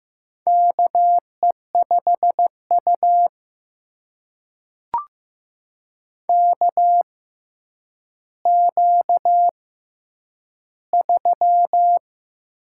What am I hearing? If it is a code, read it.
Morse code, heard as a keyed tone, about 15 wpm, spaced as for 8 wpm, 700 Hz. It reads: KE5U KQ3
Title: USA Call Signs - 4 Characters - 6x Character Spacing - 15wpm